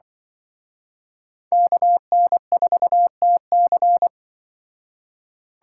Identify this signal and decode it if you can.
Morse code, heard as a keyed tone, about 24 wpm, 700 Hz. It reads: KN4TC
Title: Speed Racing - USA 2x2 Call Signs 20wpm